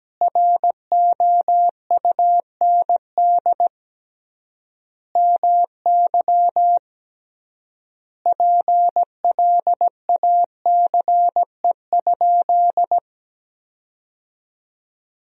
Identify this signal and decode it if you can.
Morse code, heard as a keyed tone, about 17 wpm, 700 Hz. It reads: ROUND MY PLACE?
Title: 3x Word-Spacing - Sentences from Top 200 Words - 17wpm